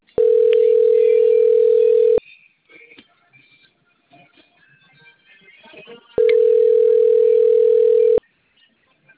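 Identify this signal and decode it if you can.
Ringback tone (North America), 440 + 480 Hz.
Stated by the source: Outgoing call recording from the Cornelio Procopio phone system.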